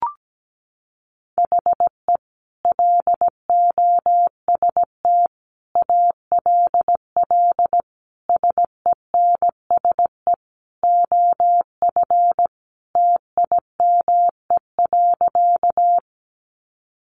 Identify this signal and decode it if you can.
Morse code, heard as a keyed tone, about 17 wpm, 700 Hz. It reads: HE LOST ALL SENSE OF TIME.
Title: Sentences from Top 1400 Words 17wpm